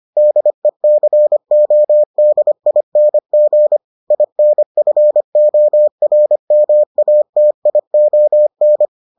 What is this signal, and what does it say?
Morse code, heard as a keyed tone, about 25 wpm, 600 Hz. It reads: DECODING INFORMATION